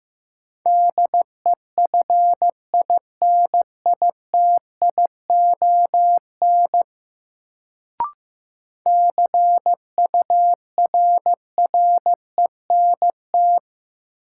Morse code, heard as a keyed tone, about 15 wpm, 700 Hz. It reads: DEFINITION CURRENT
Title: Sentences from Top 1000 Words - Review Part 4 - New Word Drill - 15wpm